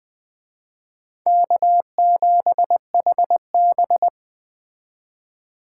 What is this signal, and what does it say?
Morse code, heard as a keyed tone, about 20 wpm, 700 Hz. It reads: K7HB
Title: Speed Racing - USA 1x2 Call Signs 17wpm